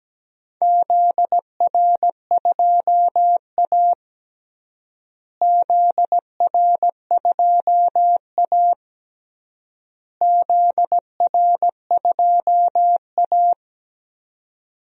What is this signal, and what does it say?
Morse code, heard as a keyed tone, about 17 wpm, 700 Hz. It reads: ZR2A ZR2A ZR2A
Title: DXCC-100 Call Signs - 4 Characters - Repeated 3x - 17wpm